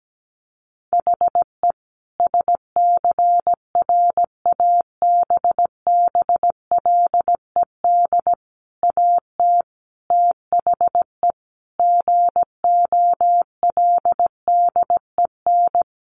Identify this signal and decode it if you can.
Morse code, heard as a keyed tone, about 17 wpm, 700 Hz. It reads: HE SCRABBLED AT THE GOLDEN
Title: A Sound Of Thunder 17wpm